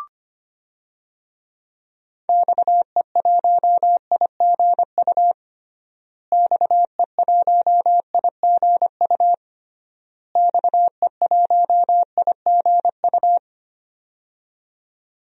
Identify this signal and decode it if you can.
Morse code, heard as a keyed tone, about 25 wpm, 700 Hz. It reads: XE1IGU XE1IGU XE1IGU
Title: DXCC-100 Call Signs - 6 Characters - Repeated 3x - 25wpm